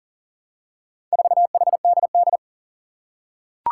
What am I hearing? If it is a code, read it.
Morse code, heard as a keyed tone, about 40 wpm, 700 Hz. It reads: 4HDD
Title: Four Challenging Letter-Number Combinations 40wpm